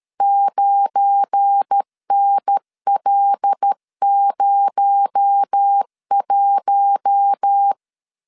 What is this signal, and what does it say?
Morse code, heard as a keyed tone, about 13 wpm, 800 Hz. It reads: 9NL01
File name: LEZIONI / 7 GF9Z / 7 GF9 12wpm 2319